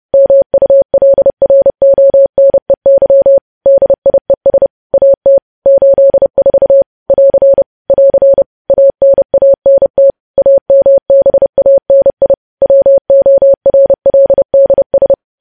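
Morse code, heard as a keyed tone, about 30 wpm, 550 Hz. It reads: MULRONEY DIES AT 84 <AR> <AR> ANANT AMBANI WORLDS
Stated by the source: Morse news headlines at 30 wpm.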